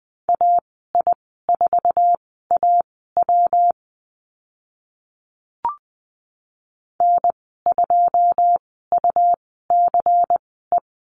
Morse code, heard as a keyed tone, about 20 wpm, spaced as for 14 wpm, 700 Hz. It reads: AI4AW N2UCE
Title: 140 Lesson 23g - Letter f - Call Signs 20wpm